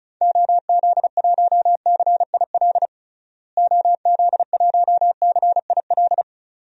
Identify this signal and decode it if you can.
Morse code, heard as a keyed tone, about 36 wpm, 700 Hz. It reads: OZ1CIL OZ1CIL
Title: DXCC-100 Call Signs - 6 Characters - Repeated 2x - 35wpm